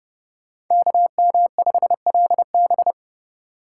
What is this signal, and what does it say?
Morse code, heard as a keyed tone, about 28 wpm, 700 Hz. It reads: KM5LB